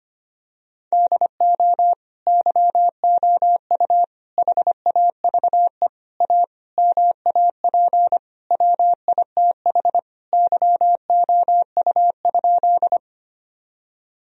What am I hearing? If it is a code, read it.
Morse code, heard as a keyed tone, about 25 wpm, 700 Hz. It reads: DO YOU HAVE A MAP WITH YOU?
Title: Sentences from Top 1000 Words - Review Part 2 - 25wpm